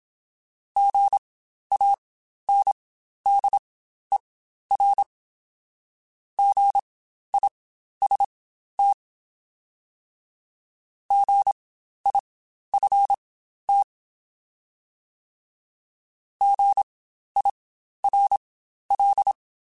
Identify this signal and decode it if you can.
Morse code, heard as a keyed tone, about 27 wpm, spaced as for 12 wpm, 800 Hz. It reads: GANDER GIST GIFT GIRL
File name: Spaced Alphas / Spaced Alphas mp3s / Gs sp @25